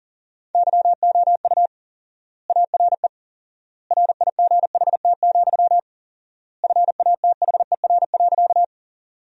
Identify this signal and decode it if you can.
Morse code, heard as a keyed tone, about 37 wpm, 700 Hz. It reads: YOU ARE RIGHT, FATHER.
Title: 4x Word-Spacing - Sentences from Top 300 Words - 40wpm